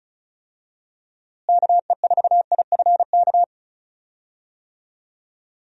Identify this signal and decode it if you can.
Morse code, heard as a keyed tone, about 34 wpm, 700 Hz. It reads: KE4IFK
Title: USA Call Signs - 6 Characters - Repeated 3x - 35wpm